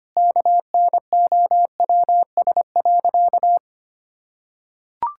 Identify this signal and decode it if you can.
Morse code, heard as a keyed tone, about 25 wpm, 700 Hz. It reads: KNOWS.
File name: practice / 25wpm / Sentences with 4-Word Phrases - 25wpm